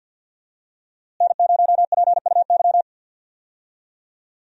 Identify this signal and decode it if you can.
Morse code, heard as a keyed tone, about 46 wpm, 700 Hz. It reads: N0PUY
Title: USA Call Signs - 5 Characters - Repeated 2x - 50wpm